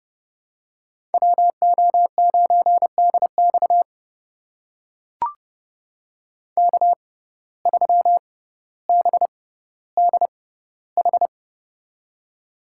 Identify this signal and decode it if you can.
Morse code, heard as a keyed tone, about 30 wpm, spaced as for 15 wpm, 700 Hz. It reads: WO9DX K3BDH
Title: USA Call Signs - 5 Characters - 6x Character Spacing - 30wpm